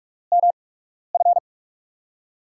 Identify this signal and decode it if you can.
Morse code, heard as a keyed tone, about 47 wpm, spaced as for 12 wpm, 700 Hz. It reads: MF
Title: USA Call Signs - 8x Character Spacing - 45wpm